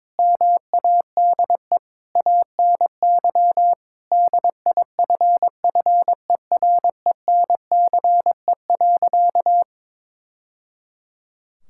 Morse code, heard as a keyed tone, about 21 wpm, 700 Hz. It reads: MADE ANY DIFFERENCE.
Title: Sentences from Top 1000 Words - Review Part 5 - 22wpm